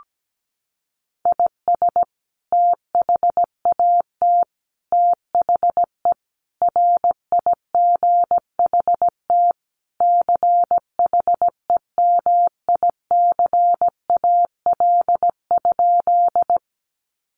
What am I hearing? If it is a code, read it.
Morse code, heard as a keyed tone, about 17 wpm, 700 Hz. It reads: IS THAT THE RIGHT CHEMICAL?
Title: Sentences from Top 1200 Words 17wpm